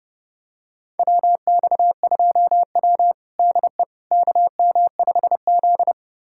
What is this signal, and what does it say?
Morse code, heard as a keyed tone, about 30 wpm, 700 Hz. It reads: WX2W DE KM5Z
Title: USA Callsign DE Callsign - 4 Characters - 30wpm